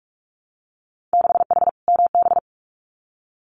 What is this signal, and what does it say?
Morse code, heard as a keyed tone, about 46 wpm, 700 Hz. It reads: 6H MB
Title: Winter Field Day Exchanges Encore 45wpm